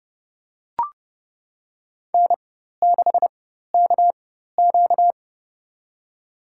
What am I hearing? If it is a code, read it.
Morse code, heard as a keyed tone, about 30 wpm, spaced as for 14 wpm, 700 Hz. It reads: N6KQ
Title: USA Call Signs - 4 Characters - 4x Character Spacing - 30wpm